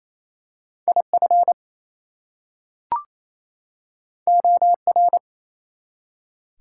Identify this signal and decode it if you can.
Morse code, heard as a keyed tone, about 27 wpm, 700 Hz. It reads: IF OR
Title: Two Letter Words - 28wpm